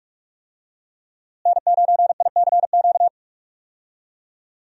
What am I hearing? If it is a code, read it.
Morse code, heard as a keyed tone, about 41 wpm, 700 Hz. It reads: N9ICQ
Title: USA Call Signs - 5 Characters - Repeated 2x - 45wpm